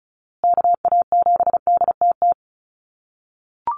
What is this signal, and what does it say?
Morse code, heard as a keyed tone, about 34 wpm, 700 Hz. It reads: KA7DTT
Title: USA Call Signs Repeated 2x 35wpm